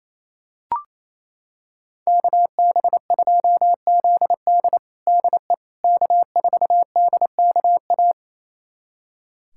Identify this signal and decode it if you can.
Morse code, heard as a keyed tone, about 28 wpm, 700 Hz. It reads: KB2ZD DE K4DKA
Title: USA Callsign DE Callsign - 5 Characters - 28wpm